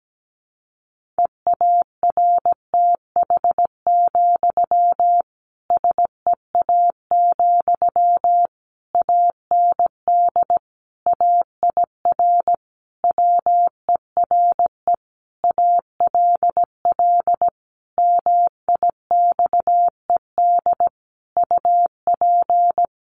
Morse code, heard as a keyed tone, about 17 wpm, 700 Hz. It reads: EARTH, SEA, AND AIR WERE ALL MIXED UP
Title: Sentences from Top 1500 Words 17wpm